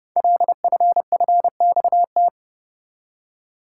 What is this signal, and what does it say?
Morse code, heard as a keyed tone, about 30 wpm, 700 Hz. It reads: LFFXT
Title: Code Groups - 5 Letters - 30wpm